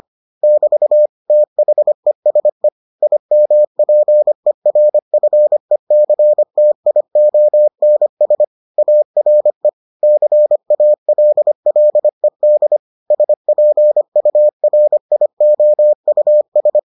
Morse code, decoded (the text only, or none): <BT> THESE IMPERFECTIONS ARE CALLED SPURIOUS